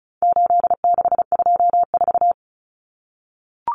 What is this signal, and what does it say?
Morse code, heard as a keyed tone, about 34 wpm, 700 Hz. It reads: 8624
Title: Common Cut Numbers - 4-Digit 35wpm